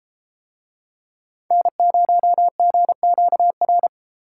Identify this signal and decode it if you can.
Morse code, heard as a keyed tone, about 31 wpm, 700 Hz. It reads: N0GQR